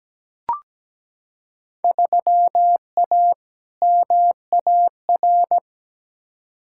Morse code, heard as a keyed tone, about 17 wpm, 700 Hz. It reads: 3A MAR
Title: Field Day Exchanges Encore - 17wpm